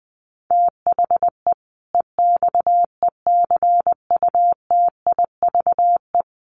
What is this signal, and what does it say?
Morse code, heard as a keyed tone, about 20 wpm, 700 Hz. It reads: THE EXECUTIVE
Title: Sets of 5 Words Encore 20wpm